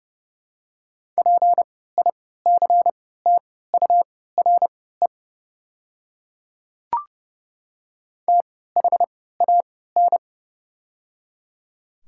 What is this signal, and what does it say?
Morse code, heard as a keyed tone, about 31 wpm, spaced as for 17 wpm, 700 Hz. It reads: PICTURE THAN
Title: 3x Character Spacing - Top 300 Words - 30wpm